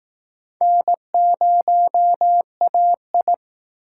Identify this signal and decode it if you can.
Morse code, heard as a keyed tone, about 20 wpm, 700 Hz. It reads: N0AI